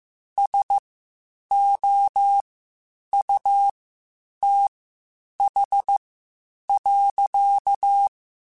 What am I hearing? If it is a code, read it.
Morse code, heard as a keyed tone, about 15 wpm, spaced as for 8 wpm, 800 Hz. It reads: SOUTH.
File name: Spaced Characters / Spaced Characters-mp3s / Weather 4 21 sp @13